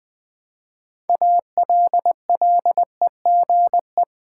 Morse code, heard as a keyed tone, about 20 wpm, 700 Hz. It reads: ALLEGE
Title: Build A Word - 3 Letters At A Time - 6-Letter Words - 20wpm